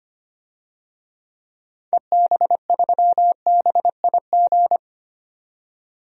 Morse code, heard as a keyed tone, about 25 wpm, 700 Hz. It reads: EB3BIG
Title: DXCC-100 Call Signs - 6 Characters - 25wpm